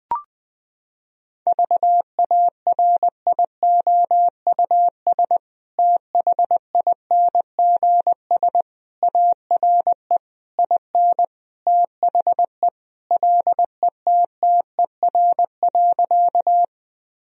Morse code, heard as a keyed tone, about 21 wpm, 700 Hz. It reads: VARIOUS THINGS ARE IN THE LETTER.
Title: Sentences from Top 1000 Words - Review Part 4 - 20wpm